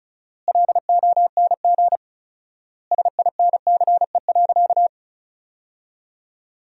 Morse code, heard as a keyed tone, about 36 wpm, 700 Hz. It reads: LONG SINCE.
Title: 4x Word-Spacing - Sentences from Top 300 Words - 35wpm